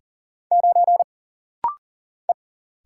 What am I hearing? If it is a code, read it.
Morse code, heard as a keyed tone, about 42 wpm, 700 Hz. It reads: 9 E
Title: Single Letter-Number - Rapid-Fire 40wpm